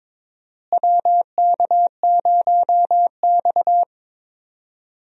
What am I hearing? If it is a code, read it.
Morse code, heard as a keyed tone, about 22 wpm, 700 Hz. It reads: WK0X
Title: USA Call Signs - 4 Characters - Repeated 2x - 22wpm